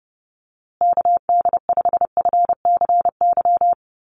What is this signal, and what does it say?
Morse code, heard as a keyed tone, about 30 wpm, 700 Hz. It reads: KD5FCY
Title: USA Call Signs Repeated 2x 30wpm